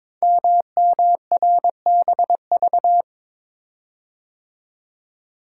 Morse code, heard as a keyed tone, about 22 wpm, 700 Hz. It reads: MMRBV
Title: Code Groups - 5 Letters - 22wpm